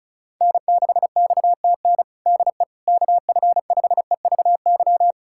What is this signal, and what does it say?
Morse code, heard as a keyed tone, about 36 wpm, 690 Hz. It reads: N6XTN DE KF5EVY
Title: USA Callsign DE Callsign 35wpm